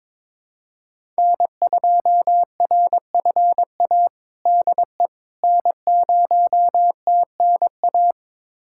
Morse code, heard as a keyed tone, about 21 wpm, 700 Hz. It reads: N2RFA DE N0TNA
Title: USA Callsign DE Callsign - 5 Characters - 22wpm